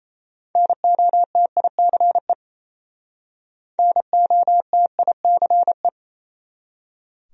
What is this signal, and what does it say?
Morse code, heard as a keyed tone, about 37 wpm, 700 Hz. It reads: NOTICE NOTICE